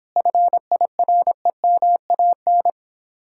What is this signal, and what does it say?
Morse code, heard as a keyed tone, about 29 wpm, 700 Hz. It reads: FIREMAN